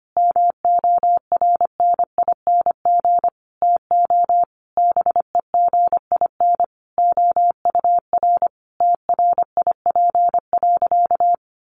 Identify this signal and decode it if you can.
Morse code, heard as a keyed tone, about 25 wpm, 700 Hz. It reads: MORNING TO BEGIN OUR TRIP.
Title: Sentences from Top 1300 Words 25wpm